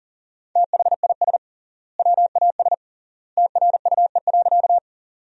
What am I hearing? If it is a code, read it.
Morse code, heard as a keyed tone, about 40 wpm, 700 Hz. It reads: THIS WAS TRUE.
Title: 3x Word-Spacing - Sentences from Top 400 Words - 40wpm